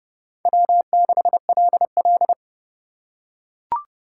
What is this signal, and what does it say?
Morse code, heard as a keyed tone, about 30 wpm, 700 Hz. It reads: W6LL
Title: USA Call Signs - 4 Characters - 2x Character Spacing - 30wpm